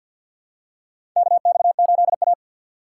Morse code, heard as a keyed tone, about 50 wpm, 700 Hz. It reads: KX8A
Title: USA Call Signs - 4 Characters - 3x Character Spacing - 50wpm